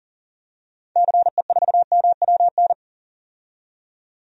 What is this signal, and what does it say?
Morse code, heard as a keyed tone, about 40 wpm, 700 Hz. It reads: CE4MWN
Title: DXCC-100 Call Signs - 6 Characters - 40wpm